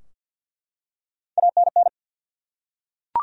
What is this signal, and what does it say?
Morse code, heard as a keyed tone, about 49 wpm, 700 Hz. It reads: ANN
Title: USA Common Names - Max 3 Letters - 50wpm